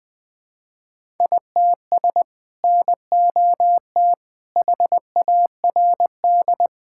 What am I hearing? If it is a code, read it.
Morse code, heard as a keyed tone, about 21 wpm, 700 Hz. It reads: ITS NOT HARD
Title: Sentences with 6-Word Phrases - 20wpm